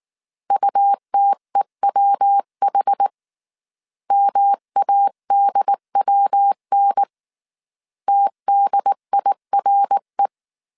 Morse code, heard as a keyed tone, about 19 wpm, 800 Hz. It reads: UTEWH MADWN TDIRE